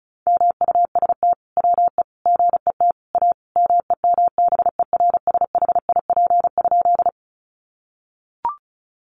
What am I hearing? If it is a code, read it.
Morse code, heard as a keyed tone, about 34 wpm, 700 Hz. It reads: MUST WE GET A MEMBERSHIP?
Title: Sentences from Top 1400 Words 35wpm